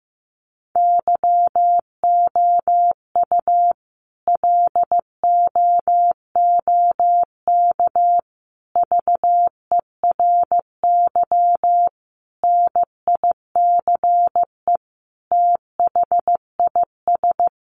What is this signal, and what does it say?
Morse code, heard as a keyed tone, about 15 wpm, 700 Hz. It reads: YOU LOOK VERY NICE THIS
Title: Sentences from Top 1500 Words 15wpm